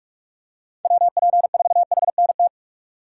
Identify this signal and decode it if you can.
Morse code, heard as a keyed tone, about 46 wpm, 700 Hz. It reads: WP4HNT
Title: USA Call Signs - 6 Characters - 4x Character Spacing - 45wpm